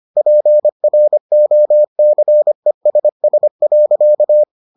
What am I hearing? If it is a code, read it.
Morse code, heard as a keyed tone, about 25 wpm, 600 Hz. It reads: PROCESS.